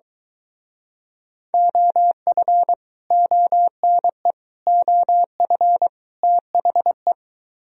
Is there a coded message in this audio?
Yes: Morse code from 1.5 to 7.1 s, heard as a keyed tone, about 23 wpm, 700 Hz. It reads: OF ONE OF THE